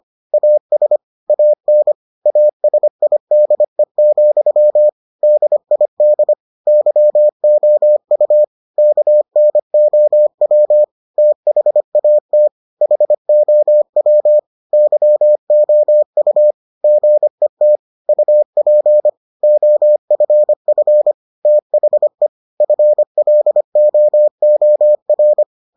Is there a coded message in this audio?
Yes: Morse code from 0.3 to 25.4 s, heard as a keyed tone, about 25 wpm, 600 Hz. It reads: AS AN ASIDE, DID YOU KNOW THAT HOW YOU GET UP OFF THE FLOOR